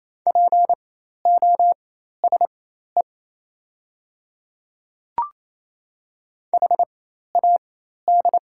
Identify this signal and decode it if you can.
Morse code, heard as a keyed tone, about 28 wpm, spaced as for 13 wpm, 700 Hz. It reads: POSE HAD
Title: 4x Character Spacing - Top 500 Words - 28wpm